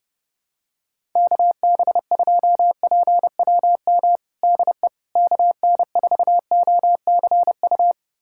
Morse code, heard as a keyed tone, about 32 wpm, 700 Hz. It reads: KB2PWM DE KN4OCU
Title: USA Callsign DE Callsign - 6 Characters - 30wpm